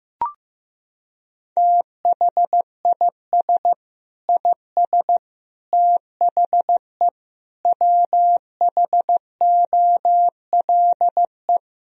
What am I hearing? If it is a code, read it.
Morse code, heard as a keyed tone, about 16 wpm, 700 Hz. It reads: THIS IS THE WHOLE
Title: Sentences from Top 1000 Words - Review Part 1 - 15wpm